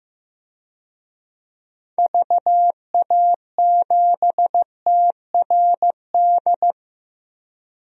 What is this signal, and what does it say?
Morse code, heard as a keyed tone, about 15 wpm, 700 Hz. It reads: VA7TRD
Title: DXCC-100 Call Signs - 6 Characters - Repeated 2x - 15wpm